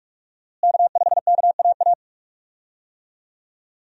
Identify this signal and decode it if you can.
Morse code, heard as a keyed tone, about 42 wpm, 700 Hz. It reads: K5KAA
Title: USA Call Signs - 5 Characters - Repeated 2x - 45wpm